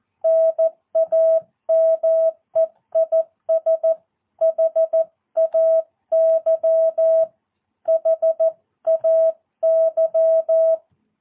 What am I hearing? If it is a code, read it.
Morse code, heard as a keyed tone, about 14 wpm, 650 Hz. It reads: NAME IS HAY HAY
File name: jh8cbh / 2022cw